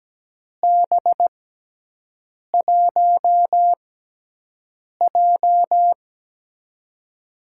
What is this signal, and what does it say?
Morse code, heard as a keyed tone, about 17 wpm, spaced as for 6 wpm, 700 Hz. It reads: B1J
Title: USA Call Signs - 4 Characters - 6x Character Spacing - 17wpm